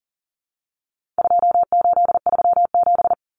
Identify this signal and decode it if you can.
Morse code, heard as a keyed tone, about 40 wpm, 700 Hz. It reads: 2837
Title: Common Cut Numbers - 4-Digit 40wpm